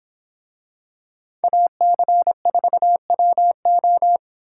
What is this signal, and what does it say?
Morse code, heard as a keyed tone, about 25 wpm, 700 Hz. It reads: AC4WO